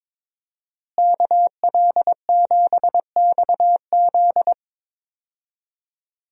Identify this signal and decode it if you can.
Morse code, heard as a keyed tone, about 22 wpm, 700 Hz. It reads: KL7XZ
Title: DXCC-100 Call Signs - 5 Characters - Repeated 3x - 22wpm